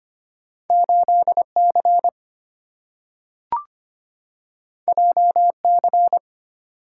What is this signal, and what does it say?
Morse code, heard as a keyed tone, about 25 wpm, 690 Hz. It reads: 8C JC
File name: Two Letter-Number Combinations - 25wpm